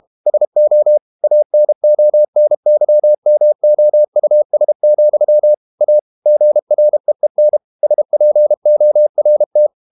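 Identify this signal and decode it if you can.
Morse code, heard as a keyed tone, about 32 wpm, 600 Hz. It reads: SO ANONYMOUS, A GREEN SPORT
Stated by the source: Morse code practice at 32 wpm from CW med Gnister, reading Ken Follett's novel Never in English.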